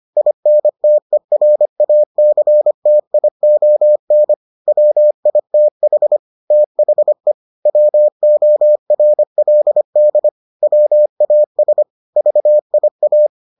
Morse code, heard as a keyed tone, about 25 wpm, 600 Hz. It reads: INTERACTION WITH THE WORLD WAS VIA